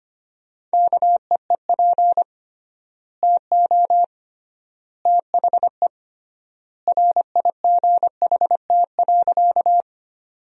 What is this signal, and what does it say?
Morse code, heard as a keyed tone, about 25 wpm, 700 Hz. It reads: KEEP TO THE RIGHT.
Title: 3x Word-Spacing - Sentences from Top 300 Words - 25wpm